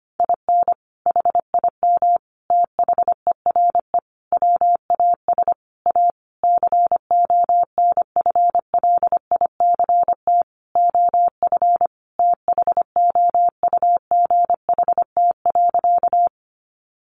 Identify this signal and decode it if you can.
Morse code, heard as a keyed tone, about 25 wpm, 700 Hz. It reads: IN HIM THERE WAS A CONFLICT OF THOUGHT.
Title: Sentences from Top 1300 Words 25wpm